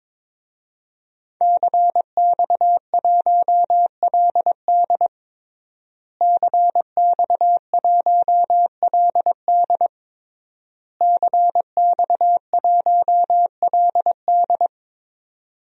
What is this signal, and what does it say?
Morse code, heard as a keyed tone, about 22 wpm, 700 Hz. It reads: CX1LD CX1LD CX1LD
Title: DXCC-100 Call Signs - 5 Characters - Repeated 3x - 22wpm